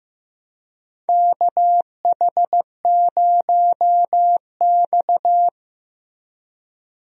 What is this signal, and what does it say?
Morse code, heard as a keyed tone, about 15 wpm, 700 Hz. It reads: KH0X
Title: USA Call Signs - 4 Characters - Repeated 2x - 15wpm